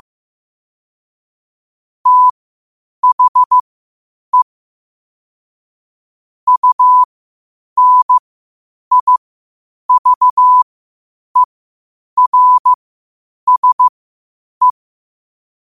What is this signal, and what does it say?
Morse code, heard as a keyed tone, about 15 wpm, spaced as for 8 wpm, 1000 Hz. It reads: THE UNIVERSE
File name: qotdfiles / 8wpm